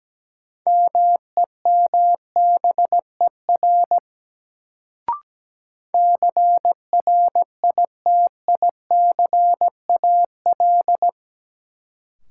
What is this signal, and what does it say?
Morse code, heard as a keyed tone, about 17 wpm, 700 Hz. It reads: MEMBER CRITICAL
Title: Sentences from Top 1100 Words - New Word Drill - 17wpm